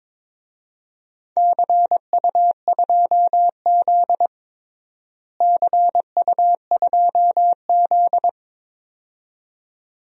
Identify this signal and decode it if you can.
Morse code, heard as a keyed tone, about 21 wpm, 700 Hz. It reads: CU2Z CU2Z
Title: DXCC-100 Call Signs - 4 Characters - Repeated 2x - 22wpm